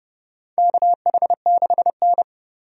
Morse code, heard as a keyed tone, about 30 wpm, 700 Hz. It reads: KH6N
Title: DXCC-100 Call Signs - 4 Characters - 30wpm